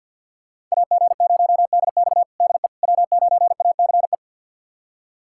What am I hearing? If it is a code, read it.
Morse code, heard as a keyed tone, about 46 wpm, 700 Hz. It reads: AG0DX DE W9ACE
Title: USA Callsign DE Callsign - 5 Characters - 50wpm